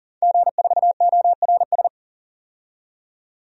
Morse code, heard as a keyed tone, about 40 wpm, 700 Hz. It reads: G4ORS
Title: DXCC-100 Call Signs - 5 Characters - 40wpm